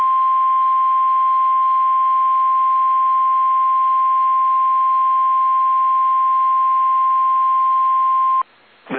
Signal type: Emergency Alert System (SAME) tones